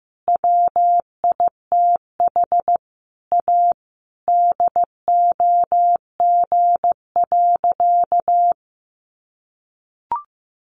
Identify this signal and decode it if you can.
Morse code, heard as a keyed tone, about 15 wpm, 700 Hz. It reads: WITH A DOG.
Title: Additional Easy Phrases 15wpm